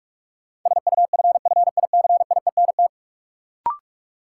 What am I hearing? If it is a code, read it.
Morse code, heard as a keyed tone, about 47 wpm, 700 Hz. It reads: SUFFICIENT
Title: Sentences from Top 1400 Words - New Word Drill Alt - 45wpm